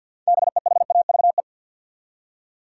Morse code, heard as a keyed tone, about 47 wpm, 700 Hz. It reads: BEHAVE
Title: Six Letter Words - 50wpm